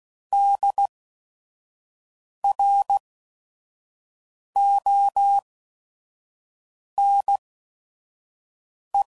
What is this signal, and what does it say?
Morse code, heard as a keyed tone, about 16 wpm, spaced as for 5 wpm, 800 Hz. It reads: DRONE
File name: Prefixes / By First Letter / Ds